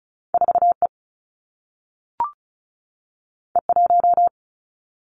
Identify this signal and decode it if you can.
Morse code, heard as a keyed tone, about 36 wpm, 700 Hz. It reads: 4E E1
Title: Two Challenging Letter-Number Combinations 35wpm